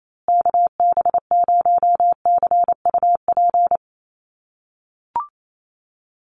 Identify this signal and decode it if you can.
Morse code, heard as a keyed tone, about 28 wpm, 700 Hz. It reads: KB0CUP
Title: USA Call Signs Repeated 3x 28wpm